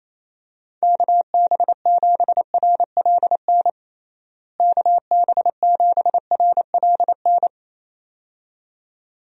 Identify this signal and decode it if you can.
Morse code, heard as a keyed tone, about 28 wpm, 700 Hz. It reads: KB7RLN KB7RLN
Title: USA Call Signs - 6 Characters - Repeated 2x - 28wpm